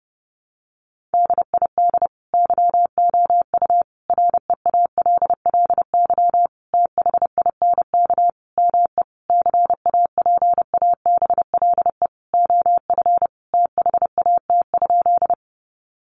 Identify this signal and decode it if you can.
Morse code, heard as a keyed tone, about 28 wpm, 700 Hz. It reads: DID YOU REALLY THINK ME CAPABLE OF THAT?
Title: Sentences from Top 1200 Words 30wpm